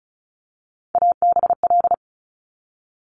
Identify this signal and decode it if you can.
Morse code, heard as a keyed tone, about 36 wpm, 700 Hz. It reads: ABL
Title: Top Three Letter Combinations in Words 35wpm